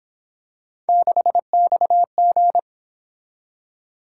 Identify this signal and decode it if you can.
Morse code, heard as a keyed tone, about 28 wpm, 700 Hz. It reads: 6XG